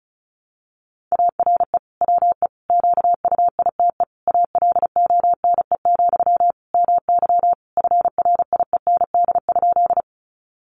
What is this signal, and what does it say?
Morse code, heard as a keyed tone, about 34 wpm, 700 Hz. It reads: ARE WE QUITE ALONE, MY FRIEND?
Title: Sentences from Top 1000 Words 35wpm